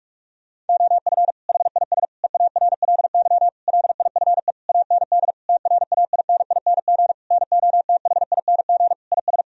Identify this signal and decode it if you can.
Morse code, heard as a keyed tone, about 47 wpm, 700 Hz. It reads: OF HIS EARLY LIFE AND TRAINING NOTHING IS
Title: Sentences from Top 1000 Words - Review Part 4 - 45wpm